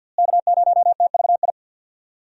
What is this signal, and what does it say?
Morse code, heard as a keyed tone, about 50 wpm, 700 Hz. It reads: K0TVI
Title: USA Call Signs - 5 Characters - Repeated 3x - 50wpm